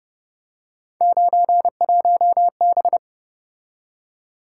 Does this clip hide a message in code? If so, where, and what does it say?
Yes: Morse code from 1.0 to 3.0 s, heard as a keyed tone, about 31 wpm, 700 Hz. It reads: 91B